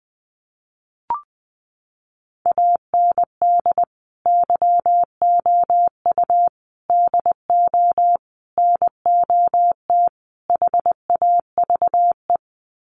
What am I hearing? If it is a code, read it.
Morse code, heard as a keyed tone, about 20 wpm, 700 Hz. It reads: AND YOU DO NOT HAVE
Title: Sets of 5 Words Encore 20wpm